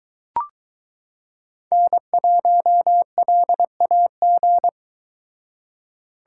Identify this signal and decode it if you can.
Morse code, heard as a keyed tone, about 23 wpm, 700 Hz. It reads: N1LAG